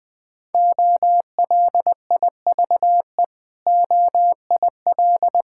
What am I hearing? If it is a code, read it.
Morse code, heard as a keyed tone, about 20 wpm, 700 Hz. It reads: OLIVE OIL
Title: Difficult Words with Hints - 20wpm